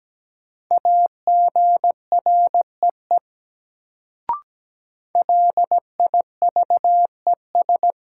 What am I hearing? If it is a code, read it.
Morse code, heard as a keyed tone, about 17 wpm, 700 Hz. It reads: AGREE LIVES
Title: Sets of 5 Words Encore - New Word Drill - 17wpm